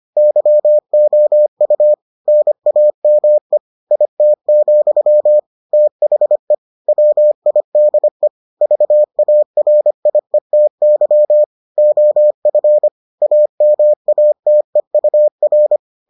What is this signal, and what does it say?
Morse code, heard as a keyed tone, about 25 wpm, 600 Hz. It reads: YOU NAME IT, THE WIDE VARIETY OF AMATEUR